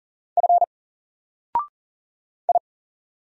Morse code, heard as a keyed tone, about 40 wpm, 700 Hz. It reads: F I
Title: Single Character (Letter:Number:Punctuation:Prosign) Alt 40wpm